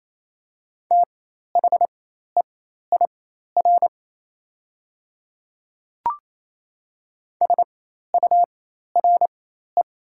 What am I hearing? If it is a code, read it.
Morse code, heard as a keyed tone, about 28 wpm, spaced as for 13 wpm, 700 Hz. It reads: THEIR SURE
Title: 4x Character Spacing - Top 300 Words - 28wpm